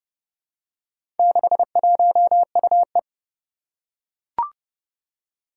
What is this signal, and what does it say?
Morse code, heard as a keyed tone, about 28 wpm, 700 Hz. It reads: 61UE
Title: Four Challenging Letter-Number Combinations 30wpm